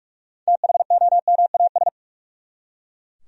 Morse code, heard as a keyed tone, about 43 wpm, 700 Hz. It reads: THOMAS
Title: USA Common Names - 45wpm